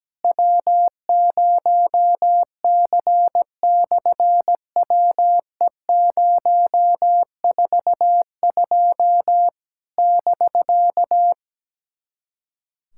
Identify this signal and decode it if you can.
Morse code, heard as a keyed tone, about 17 wpm, 700 Hz. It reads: W0C/WE042 <BK>
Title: SOTA QSO Practice 17wpm